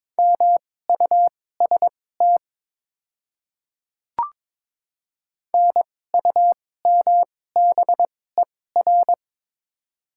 Morse code, heard as a keyed tone, about 22 wpm, spaced as for 16 wpm, 700 Hz. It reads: MUST NUMBER
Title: 2x Character Spacing - Top 200 Words - 22wpm